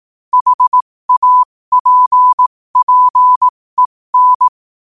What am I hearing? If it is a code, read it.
Morse code, heard as a keyed tone, about 18 wpm, spaced as for 15 wpm, 1000 Hz. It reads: HAPPEN
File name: qotdfiles / 15wpm